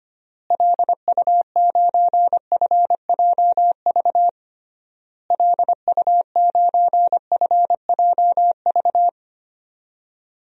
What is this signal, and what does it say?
Morse code, heard as a keyed tone, about 25 wpm, 700 Hz. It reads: LU9FJV LU9FJV
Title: DXCC-100 Call Signs - 6 Characters - Repeated 3x - 25wpm